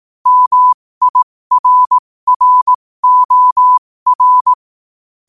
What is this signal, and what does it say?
Morse code, heard as a keyed tone, about 18 wpm, spaced as for 15 wpm, 1000 Hz. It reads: MIRROR
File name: qotdfiles / 15wpm